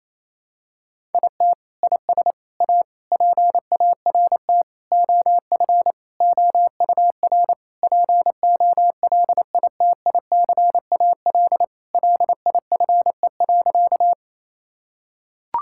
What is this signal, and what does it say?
Morse code, heard as a keyed tone, about 28 wpm, 700 Hz. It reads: IT IS A PART OF OUR POLITICAL LIFE.
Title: Sentences from Top 1000 Words - Review Part 5 - 28wpm